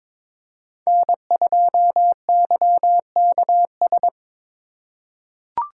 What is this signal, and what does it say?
Morse code, heard as a keyed tone, about 21 wpm, 700 Hz. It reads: N2YKS
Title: USA Call Signs - 5 Characters - 3x Character Spacing - 22wpm